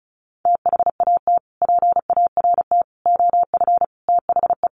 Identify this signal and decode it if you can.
Morse code, heard as a keyed tone, about 34 wpm, 700 Hz. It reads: THAT PART OF THE
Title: Sets of 5 Words Encore2 35wpm